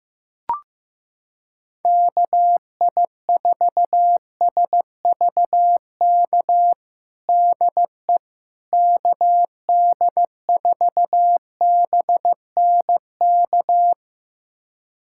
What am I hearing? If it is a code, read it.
Morse code, heard as a keyed tone, about 16 wpm, 700 Hz. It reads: KI4SVK DE KD4BNK
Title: USA Callsign DE Callsign - 6 Characters - 15wpm